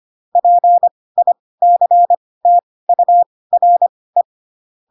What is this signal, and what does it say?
Morse code, heard as a keyed tone, about 25 wpm, spaced as for 18 wpm, 700 Hz. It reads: PICTURE